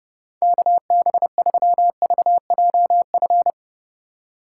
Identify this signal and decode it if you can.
Morse code, heard as a keyed tone, about 30 wpm, 700 Hz. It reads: KB3VJF